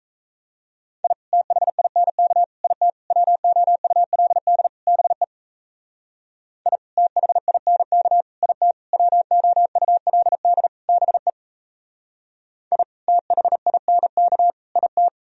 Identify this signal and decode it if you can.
Morse code, heard as a keyed tone, about 35 wpm, 700 Hz. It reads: I THINK IT WOULD BE I THINK IT WOULD BE I THINK IT